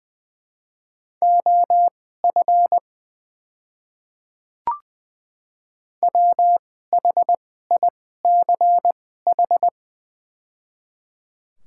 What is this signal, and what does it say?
Morse code, heard as a keyed tone, about 21 wpm, spaced as for 15 wpm, 700 Hz. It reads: OF WHICH
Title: 2x Character Spacing - Top 100 Words - 20wpm